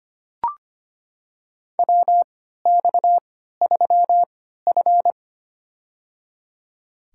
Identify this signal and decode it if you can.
Morse code, heard as a keyed tone, about 25 wpm, spaced as for 14 wpm, 700 Hz. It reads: WX3F
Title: USA Call Signs - 4 Characters - 3x Character Spacing - 25wpm